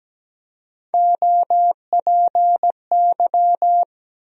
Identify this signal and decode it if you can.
Morse code, heard as a keyed tone, about 17 wpm, 700 Hz. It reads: OPY
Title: USA Call Sign Postfixes - 3 Letters 17wpm